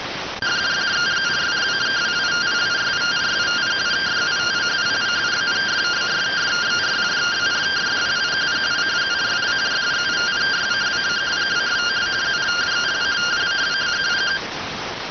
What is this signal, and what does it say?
Two-tone teleprinter signal, about 170 Hz shift.
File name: findings / CT AXIAL -377.gen-10 rx